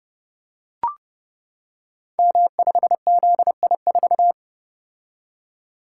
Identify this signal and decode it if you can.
Morse code, heard as a keyed tone, about 30 wpm, 700 Hz. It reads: M5ZI4
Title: Code Groups - 5 Letter:Numbers - 30wpm